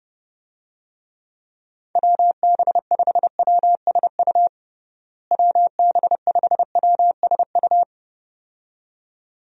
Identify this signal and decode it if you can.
Morse code, heard as a keyed tone, about 30 wpm, 700 Hz. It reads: WB5WSU WB5WSU
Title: USA Call Signs - 6 Characters - Repeated 2x - 30wpm